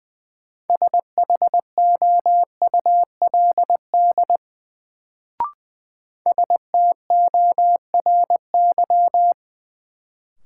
Morse code, heard as a keyed tone, about 20 wpm, 700 Hz. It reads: SHOULD STORY
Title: Sentences from Top 300 Words - New Word Drill - 20wpm